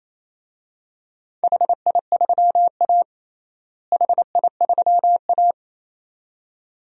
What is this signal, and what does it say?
Morse code, heard as a keyed tone, about 28 wpm, 700 Hz. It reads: HI3A HI3A
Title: DXCC-100 Call Signs - 4 Characters - Repeated 2x - 28wpm